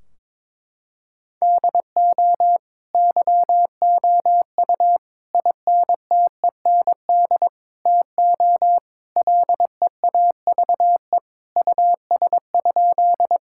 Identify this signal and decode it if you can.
Morse code, heard as a keyed tone, about 22 wpm, 700 Hz. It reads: DO YOU INTEND TO LEAVE US?
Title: Sentences from Top 1600 Words 22wpm